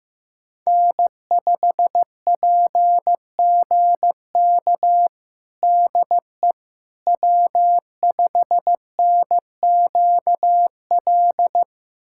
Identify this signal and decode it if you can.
Morse code, heard as a keyed tone, about 15 wpm, 700 Hz. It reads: N5PGK DE W5NQL
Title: USA Callsign DE Callsign - 5 Characters - 15wpm